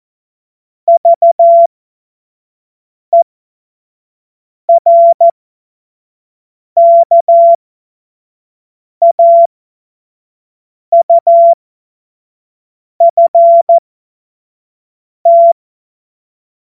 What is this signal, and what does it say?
Morse code, heard as a keyed tone, about 14 wpm, spaced as for 5 wpm, 680 Hz. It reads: VERKAUFT
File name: Morsekurs / Witze / Witz 23 Januar 2024 25 F70